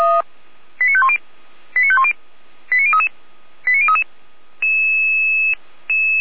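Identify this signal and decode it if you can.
Five-tone selective call (ZVEI), heard as quick runs of five tones, tones of about 70 ms. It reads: 89510 89510 89033 89033
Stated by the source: ZVEI five-tone call received with an SDR.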